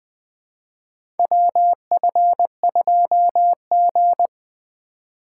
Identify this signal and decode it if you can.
Morse code, heard as a keyed tone, about 20 wpm, 700 Hz. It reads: WF2G
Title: USA Call Signs - 4 Characters - 6x Character Spacing - 20wpm